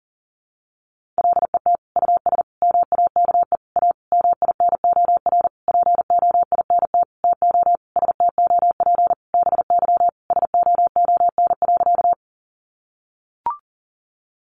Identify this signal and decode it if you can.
Morse code, heard as a keyed tone, about 40 wpm, 700 Hz. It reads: LET US MAKE A MINOR POINT TO STOP BY SOON.
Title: Sentences from Top 1500 Words 40wpm